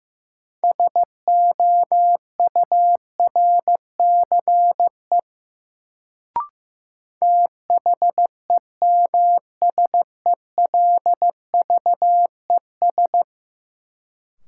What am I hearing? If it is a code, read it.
Morse code, heard as a keyed tone, about 15 wpm, 700 Hz. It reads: SOURCE THEMSELVES
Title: Sentences from Top 800 Words - New Word Drill - 15wpm